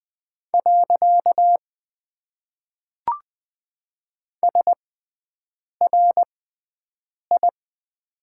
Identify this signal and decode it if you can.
Morse code, heard as a keyed tone, about 20 wpm, spaced as for 7 wpm, 700 Hz. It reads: . SRI
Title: Top 90 QSO Elements - 6x Character Spacing - 20wpm